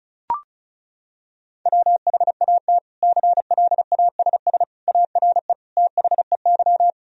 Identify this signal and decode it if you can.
Morse code, heard as a keyed tone, about 34 wpm, 700 Hz. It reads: WHAT CLASS ARE THEY
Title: Sentences from Top 1000 Words - Review Part 2 - 35wpm